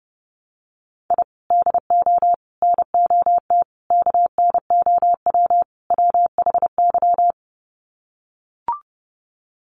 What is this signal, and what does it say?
Morse code, heard as a keyed tone, about 30 wpm, 700 Hz. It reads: I DO NOT KNOW WHY
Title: Sets of 5 Words 30wpm